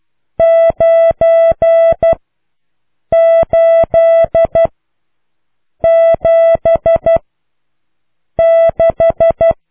Morse code, heard as a keyed tone, about 12 wpm, 660 Hz. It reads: 9 8 7 6